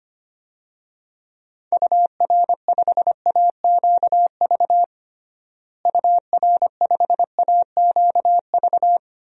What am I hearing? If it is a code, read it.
Morse code, heard as a keyed tone, about 25 wpm, 700 Hz. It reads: UR5AQV UR5AQV
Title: DXCC-100 Call Signs - 6 Characters - Repeated 2x - 25wpm